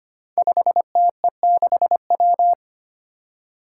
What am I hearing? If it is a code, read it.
Morse code, heard as a keyed tone, about 25 wpm, 700 Hz. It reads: 5TE6W
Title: Five Challenging Letter-Number Combinations 25wpm